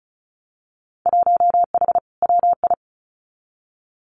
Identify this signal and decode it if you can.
Morse code, heard as a keyed tone, about 36 wpm, 700 Hz. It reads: 1H WI
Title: Winter Field Day Exchanges 35wpm